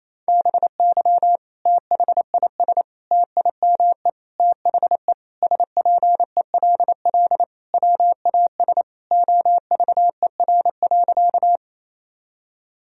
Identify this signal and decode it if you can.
Morse code, heard as a keyed tone, about 28 wpm, 700 Hz. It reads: BY THIS TIME THE SPELL WAS OVER.
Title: Sentences from Top 1000 Words - Review Part 1 - 28wpm